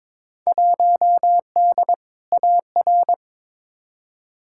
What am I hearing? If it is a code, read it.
Morse code, heard as a keyed tone, about 22 wpm, 700 Hz. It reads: 1D AR
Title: Field Day Exchanges - 22wpm